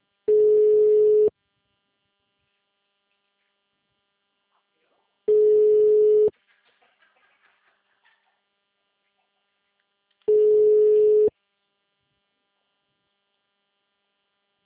Ringback tone (Europe, 425 Hz).
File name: out- -2001- - - .40780